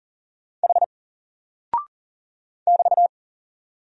Morse code, heard as a keyed tone, about 40 wpm, 700 Hz. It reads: H <BT>
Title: Single Character (Letter:Number:Punctuation:Prosign) Alt 40wpm